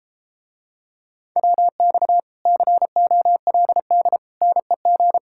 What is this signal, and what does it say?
Morse code, heard as a keyed tone, about 33 wpm, 700 Hz. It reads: WX COLD NEG